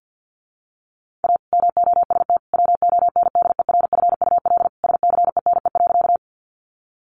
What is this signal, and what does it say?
Morse code, heard as a keyed tone, about 50 wpm, 700 Hz. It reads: A MOST WONDERFUL SCENE.
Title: Sentences from Top 1100 Words 50wpm